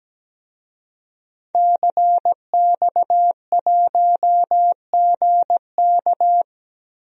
Morse code, heard as a keyed tone, about 17 wpm, 700 Hz. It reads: CX1GK
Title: DXCC-100 Call Signs - 5 Characters - 17wpm